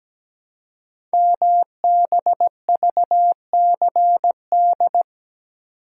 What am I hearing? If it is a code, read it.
Morse code, heard as a keyed tone, about 17 wpm, 700 Hz. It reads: MBVCD
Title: Code Groups - 5 Letters - 17wpm